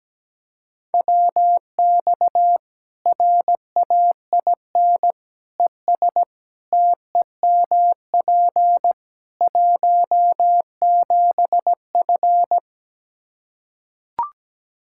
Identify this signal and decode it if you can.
Morse code, heard as a keyed tone, about 17 wpm, 700 Hz. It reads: WX RAIN ES TEMP 17F
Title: Practice QSO - Exchange 2 - Info - 17wpm